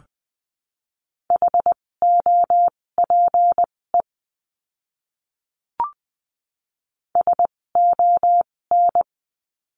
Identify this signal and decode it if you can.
Morse code, heard as a keyed tone, about 20 wpm, spaced as for 16 wpm, 700 Hz. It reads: HOPE SON
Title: 256 Lesson 42a - Words 12-20wpm